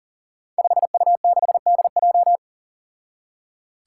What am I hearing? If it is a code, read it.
Morse code, heard as a keyed tone, about 40 wpm, 700 Hz. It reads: 5U6DJ
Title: Five Challenging Letter-Number Combinations 40wpm